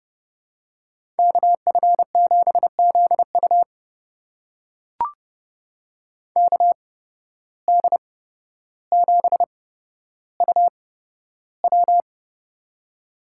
Morse code, heard as a keyed tone, about 29 wpm, spaced as for 13 wpm, 700 Hz. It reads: KF7ZU KD7UW
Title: USA Call Signs - 5 Characters - 8x Character Spacing - 30wpm